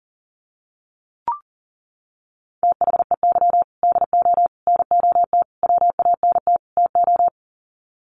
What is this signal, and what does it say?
Morse code, heard as a keyed tone, about 40 wpm, 700 Hz. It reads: THEY DO NOT WANT TO
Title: Sets of 5 Words 40wpm